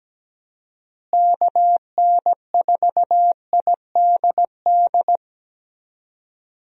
Morse code, heard as a keyed tone, about 17 wpm, 700 Hz. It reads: KN4IDD
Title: USA Call Signs - 6 Characters - Repeated 3x - 17wpm